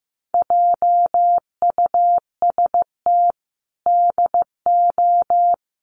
Morse code, heard as a keyed tone, about 15 wpm, 700 Hz. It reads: JUST DO
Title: Sets of 5 Words Encore 15wpm